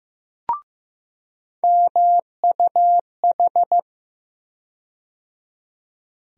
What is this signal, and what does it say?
Morse code, heard as a keyed tone, about 15 wpm, 700 Hz. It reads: MUH
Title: USA Call Sign Postfixes - 3 Letters 15wpm